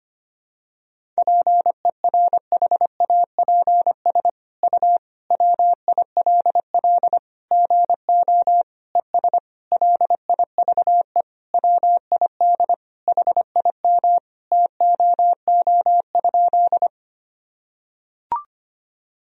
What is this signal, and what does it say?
Morse code, heard as a keyed tone, about 25 wpm, 700 Hz. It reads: PERHAPS U WILL GO ES LIVE WID HIM TOO?
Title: Sentences from Top 500 Words and Abbreviations 25wpm